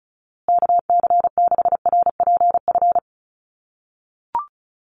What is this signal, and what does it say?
Morse code, heard as a keyed tone, about 34 wpm, 700 Hz. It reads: KC6RPF
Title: USA Call Signs Repeated 3x 35wpm